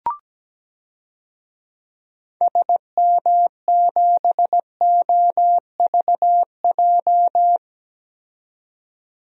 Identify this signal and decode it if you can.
Morse code, heard as a keyed tone, about 17 wpm, 700 Hz. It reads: SM7OVJ
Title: DXCC-100 Call Signs - 6 Characters - 17wpm